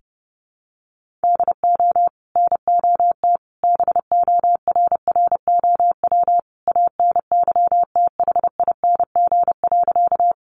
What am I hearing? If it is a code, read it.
Morse code, heard as a keyed tone, about 30 wpm, 700 Hz. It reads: DO NOT BORROW ANYTHING.
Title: Sentences from Top 1500 Words 30wpm